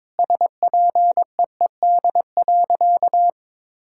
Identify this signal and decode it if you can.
Morse code, heard as a keyed tone, about 22 wpm, 700 Hz. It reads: SPEED.
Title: Sentences from Top 1000 Words - Review Part 5 - 22wpm